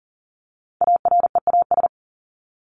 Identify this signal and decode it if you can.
Morse code, heard as a keyed tone, about 40 wpm, 700 Hz. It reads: AREAS
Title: ES:S Suffix - Top 100 Words with Base Words 40wpm